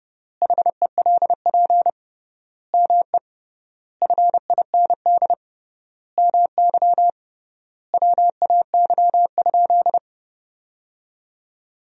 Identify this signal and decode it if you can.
Morse code, heard as a keyed tone, about 30 wpm, 700 Hz. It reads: HELP ME FIND MY WAY?
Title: 3x Word-Spacing - Sentences from Top 200 Words - 30wpm